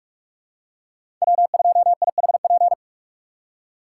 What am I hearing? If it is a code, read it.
Morse code, heard as a keyed tone, about 46 wpm, 700 Hz. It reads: W2IHP
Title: USA Call Signs - 3x Character Spacing - 45wpm